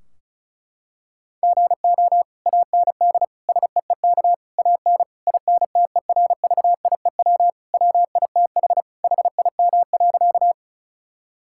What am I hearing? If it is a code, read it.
Morse code, heard as a keyed tone, about 34 wpm, 700 Hz. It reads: GO AND SEEK AN INTERVIEW WITH HIM.
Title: Sentences from Top 1600 Words 35wpm